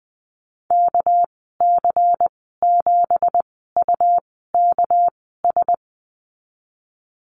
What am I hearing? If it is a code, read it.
Morse code, heard as a keyed tone, about 20 wpm, spaced as for 14 wpm, 700 Hz. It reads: KC7UKS
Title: 200 Lesson 32g - Letter k - Call Signs 20wpm